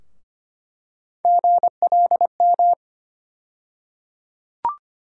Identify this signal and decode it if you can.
Morse code, heard as a keyed tone, about 25 wpm, 700 Hz. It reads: GLM
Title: USA Call Sign Postfixes - 3 Letters 25wpm